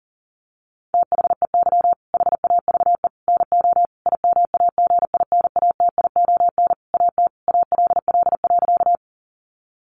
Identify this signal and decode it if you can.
Morse code, heard as a keyed tone, about 40 wpm, 700 Hz. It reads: THEY HAVE NO IMAGINATION AT ALL.
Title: Sentences from Top 1300 Words 40wpm